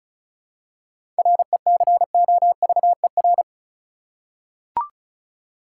Morse code, heard as a keyed tone, about 36 wpm, 700 Hz. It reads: RECOVER
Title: Seven Letter Words - 35wpm